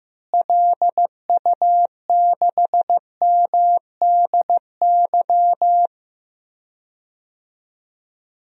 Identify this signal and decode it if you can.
Morse code, heard as a keyed tone, about 15 wpm, 700 Hz. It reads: LU6MDY
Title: DXCC-100 Call Signs - 6 Characters - Repeated 3x - 15wpm